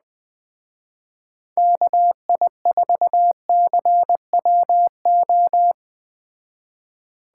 Morse code, heard as a keyed tone, about 20 wpm, 700 Hz. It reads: KI4CWO